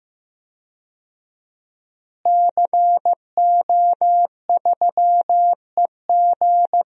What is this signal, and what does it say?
Morse code, heard as a keyed tone, about 15 wpm, 700 Hz. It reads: CO3EG
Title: DXCC-100 Call Signs - 5 Characters - Repeated 2x - 15wpm